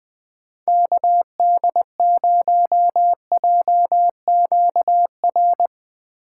Morse code, heard as a keyed tone, about 21 wpm, 700 Hz. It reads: KD0JQR
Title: Speed Racing - USA 2x3 Call Signs 17wpm